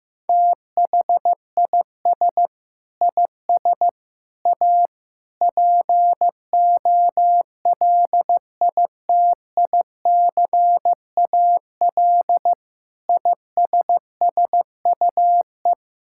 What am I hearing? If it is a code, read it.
Morse code, heard as a keyed tone, about 15 wpm, 700 Hz. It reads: THIS IS A POLITICAL ISSUE
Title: Sentences from Top 1000 Words - Review Part 6 - 15wpm